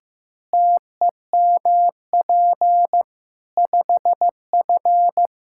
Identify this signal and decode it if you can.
Morse code, heard as a keyed tone, about 15 wpm, 700 Hz. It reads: TEMP 5F
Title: K1USN SST QSO Practice 15wpm